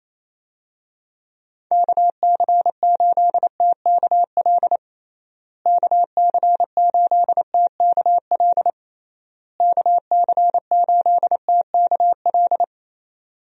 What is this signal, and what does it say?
Morse code, heard as a keyed tone, about 27 wpm, 700 Hz. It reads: KC8TKL KC8TKL KC8TKL
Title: USA Call Signs - 6 Characters - Repeated 3x - 28wpm